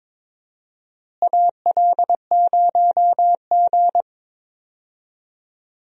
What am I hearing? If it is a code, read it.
Morse code, heard as a keyed tone, about 22 wpm, 700 Hz. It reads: AL0G
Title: USA Call Signs - 4 Characters - Repeated 2x - 22wpm